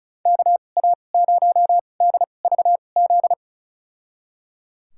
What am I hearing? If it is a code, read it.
Morse code, heard as a keyed tone, about 34 wpm, spaced as for 25 wpm, 700 Hz. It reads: KA0DVZ
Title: USA Call Signs - 6 Characters - 2x Character Spacing - 35wpm